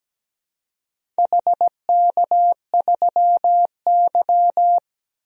Morse code, heard as a keyed tone, about 17 wpm, 700 Hz. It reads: HK3Y
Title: DXCC-100 Call Signs - 4 Characters - 17wpm